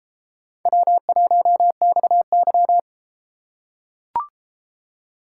Morse code, heard as a keyed tone, about 31 wpm, 700 Hz. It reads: W1XY